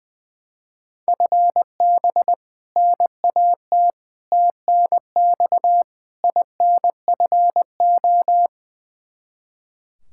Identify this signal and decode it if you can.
Morse code, heard as a keyed tone, about 20 wpm, 700 Hz. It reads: FB NAT TNX INFO
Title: Practice QSO - Exchange 3 - More Info - 20wpm